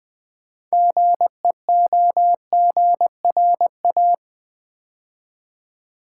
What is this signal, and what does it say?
Morse code, heard as a keyed tone, about 20 wpm, 700 Hz. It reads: GEOGRA
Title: Build A Word - 3 Letters At A Time - 9-Letter Words - 20wpm